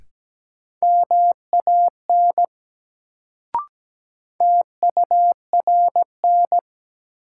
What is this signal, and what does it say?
Morse code, heard as a keyed tone, about 17 wpm, 700 Hz. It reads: MAN TURN
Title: Sentences from Top 200 Words - New Word Drill - 17wpm